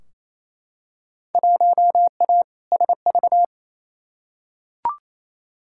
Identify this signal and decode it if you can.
Morse code, heard as a keyed tone, about 28 wpm, 700 Hz. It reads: 1A SV
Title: Field Day Exchanges Encore - 28wpm